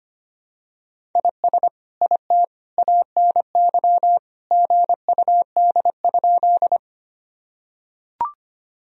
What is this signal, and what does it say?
Morse code, heard as a keyed tone, about 25 wpm, 700 Hz. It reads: IS IT ANY GUD?
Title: Sentences from Top 200 Words and Abbreviations 25wpm